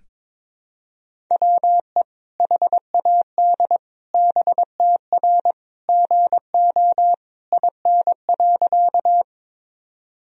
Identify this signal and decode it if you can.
Morse code, heard as a keyed tone, about 22 wpm, 700 Hz. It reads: WE HAD BTR GO IN.
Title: Sentences from Top 400 Words and Abbreviations 22wpm